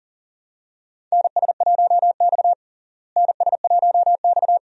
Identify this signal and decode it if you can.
Morse code, heard as a keyed tone, about 40 wpm, 700 Hz. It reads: NS1X NS1X
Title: USA Call Signs - 4 Characters - Repeated 2x - 40wpm